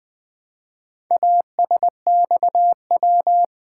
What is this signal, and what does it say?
Morse code, heard as a keyed tone, about 20 wpm, 700 Hz. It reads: ASXW
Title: Four Letter-Number Combinations - 20wpm